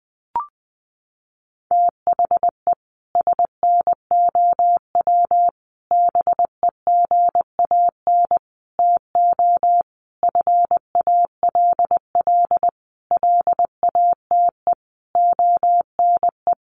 Morse code, heard as a keyed tone, about 19 wpm, 700 Hz. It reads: THE SNOW BEGAN TO FALL LATE ONE
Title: Sentences from Top 1500 Words 20wpm